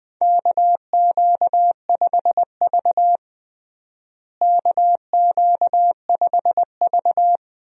Morse code, heard as a keyed tone, about 20 wpm, 700 Hz. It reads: KQ5V KQ5V
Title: USA Call Signs - 4 Characters - Repeated 2x - 20wpm